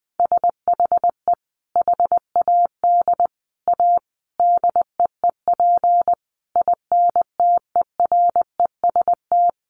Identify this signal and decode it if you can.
Morse code, heard as a keyed tone, about 20 wpm, 700 Hz. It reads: SHE HAD A DEEP INTEREST
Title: Sentences from Top 1300 Words 20wpm